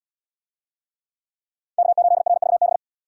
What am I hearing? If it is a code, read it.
Morse code, heard as a keyed tone, about 70 wpm, 700 Hz. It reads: K8RVB